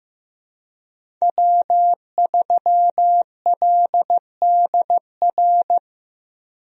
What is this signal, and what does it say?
Morse code, heard as a keyed tone, about 15 wpm, 700 Hz. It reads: W3LDR
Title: USA Call Signs - 5 Characters - Repeated 3x - 15wpm